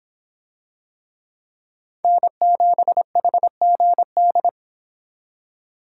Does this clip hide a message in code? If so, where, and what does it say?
Yes: Morse code from 2.0 to 4.5 s, heard as a keyed tone, about 24 wpm, 700 Hz. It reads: N7HGD